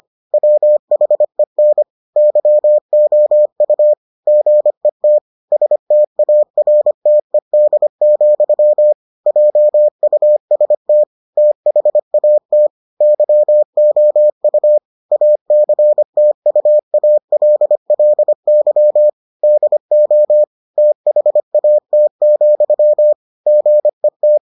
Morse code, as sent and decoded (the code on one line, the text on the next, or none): .-- .... -. / -.-- --- ..- / --. - / ... - .- .-. - . -.. --..-- / .--- ..- ... - / - .... .- - / -.-- --- ..- / .- -.-. - ..- .- .-.. .-.. -.-- / -.. --- / - .... .- - --..-- / --. -
WHEN YOU GET STARTED, JUST THAT YOU ACTUALLY DO THAT, GET